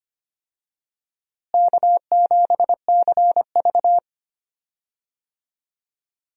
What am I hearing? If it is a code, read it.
Morse code, heard as a keyed tone, about 25 wpm, 700 Hz. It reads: K7CV
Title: USA Call Signs - 4 Characters - Repeated 3x - 25wpm